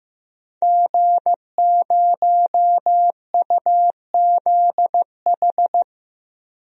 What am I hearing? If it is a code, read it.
Morse code, heard as a keyed tone, about 15 wpm, 700 Hz. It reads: G0UZH
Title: DXCC-100 Call Signs - 5 Characters - Repeated 2x - 15wpm